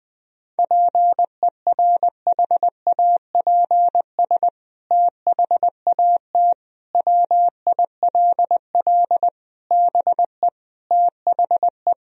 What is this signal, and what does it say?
Morse code, heard as a keyed tone, about 20 wpm, 700 Hz. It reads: PERHAPS THAT WILL BE THE
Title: Sentences from Top 1000 Words - Review Part 2 - 20wpm